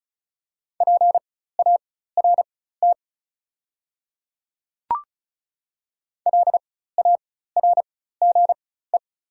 Morse code, heard as a keyed tone, about 34 wpm, spaced as for 16 wpm, 690 Hz. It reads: PART LARGE
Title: 4x Character Spacing - Top 300 Words - 35wpm